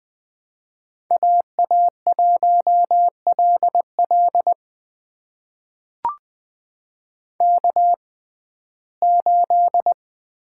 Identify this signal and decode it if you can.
Morse code, heard as a keyed tone, about 20 wpm, spaced as for 14 wpm, 700 Hz. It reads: AA1LL K8
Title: USA Call Signs - 5 Characters - 6x Character Spacing - 20wpm